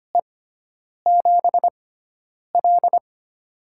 Morse code, heard as a keyed tone, about 25 wpm, spaced as for 9 wpm, 700 Hz. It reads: E7L
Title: USA Call Signs - 6x Character Spacing - 25wpm